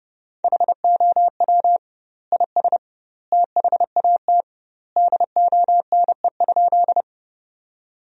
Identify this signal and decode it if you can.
Morse code, heard as a keyed tone, about 32 wpm, 700 Hz. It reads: HOW IS THAT DONE?
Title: 2x Word-Spacing - Sentences from Top 500 Words - 30wpm